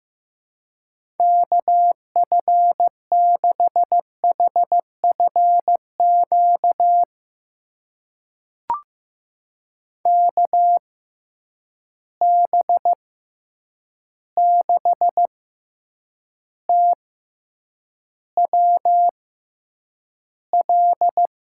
Morse code, heard as a keyed tone, about 15 wpm, spaced as for 8 wpm, 700 Hz. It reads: KF6HFQ KB6TWL
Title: USA Call Signs - 6 Characters - 6x Character Spacing - 15wpm